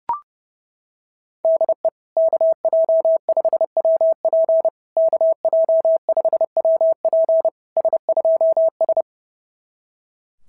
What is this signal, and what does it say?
Morse code, heard as a keyed tone, about 30 wpm, 690 Hz. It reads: DE KJ5WP KJ5WP S2S
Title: SOTA QSO Practice 30wpm